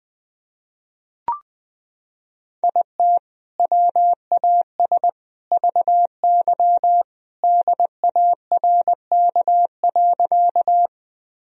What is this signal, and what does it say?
Morse code, heard as a keyed tone, about 21 wpm, 700 Hz. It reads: IT WAS VY DARK.
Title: Sentences from Top 500 Words and Abbreviations 20wpm